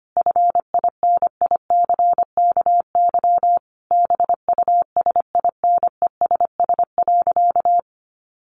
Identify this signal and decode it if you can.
Morse code, heard as a keyed tone, about 25 wpm, 700 Hz. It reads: FINICKY BUSINESS.
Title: A Sound Of Thunder 25wpm